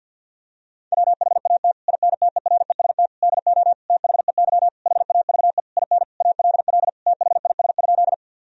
Morse code, heard as a keyed tone, about 47 wpm, 700 Hz. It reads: WHAT INTEREST DO THEY HAVE IN ALL THIS?
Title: Sentences from Top 1000 Words - Review Part 2 - 50wpm